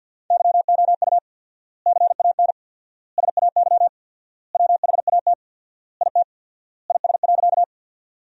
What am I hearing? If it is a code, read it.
Morse code, heard as a keyed tone, about 50 wpm, 700 Hz. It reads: YOU CAN SAY WHAT IT IS.
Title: 4x Word-Spacing - Sentences from Top 200 Words - 50wpm